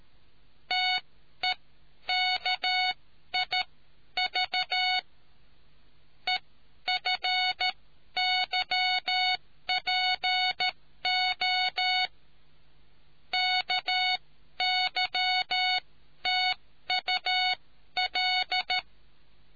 Morse code, heard as a keyed tone, about 13 wpm, spaced as for 11 wpm, 740 Hz. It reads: TEKIV EFYPO KYTUL